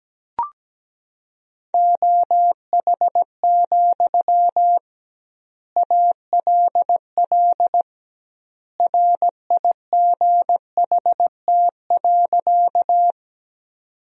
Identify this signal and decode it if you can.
Morse code, heard as a keyed tone, about 17 wpm, 700 Hz. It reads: OH, ALL RIGHT.
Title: 2x Word-Spacing - Sentences from Top 500 Words - 17wpm